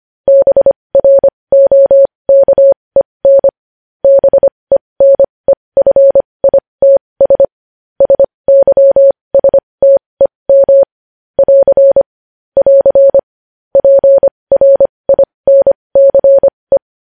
Morse code, heard as a keyed tone, about 25 wpm, spaced as for 20 wpm, 550 Hz. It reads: BROKEN BENEFITS SYSTEM <AR> <AR> PRINCE
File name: podcast / News headlines-1x-25x20